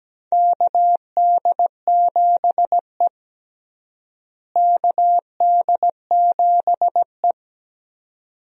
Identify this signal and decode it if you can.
Morse code, heard as a keyed tone, about 17 wpm, 700 Hz. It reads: KD7E KD7E
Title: USA Call Signs - 4 Characters - Repeated 2x - 17wpm